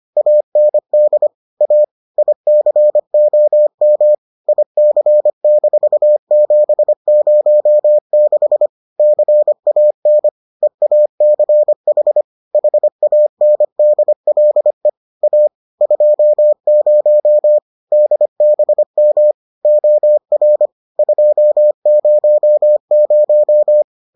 Morse code, heard as a keyed tone, about 25 wpm, 600 Hz. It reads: AND A ICOM IC-706 CAN EACH HANDLE A 20 DBM OR 200